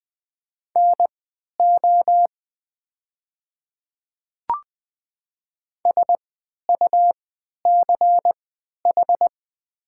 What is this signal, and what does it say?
Morse code, heard as a keyed tone, about 20 wpm, spaced as for 11 wpm, 700 Hz. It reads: NO SUCH
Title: 3x Character Spacing - Top 200 Words - 20wpm